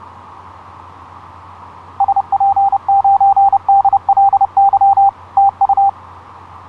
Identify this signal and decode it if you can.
Morse code, heard as a keyed tone, about 29 wpm, 790 Hz. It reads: SP9DLY TU
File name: zl1byz ha8wx etc 12m cw 06nov2010 0858